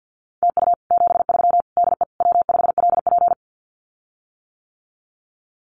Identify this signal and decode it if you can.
Morse code, heard as a keyed tone, about 50 wpm, 700 Hz. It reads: TU 73 DE W5LP
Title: Common QSO Phrases Encore 50wpm